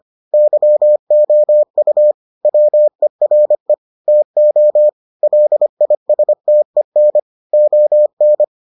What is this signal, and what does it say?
Morse code, heard as a keyed tone, about 25 wpm, 600 Hz. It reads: YOU WERE TO LISTEN ON